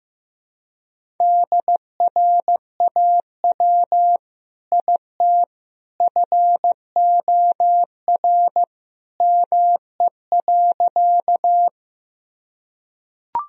Morse code, heard as a keyed tone, about 16 wpm, 700 Hz. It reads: DRAW IT FOR ME.
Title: Sentences from Top 1000 Words - Review Part 1 - 15wpm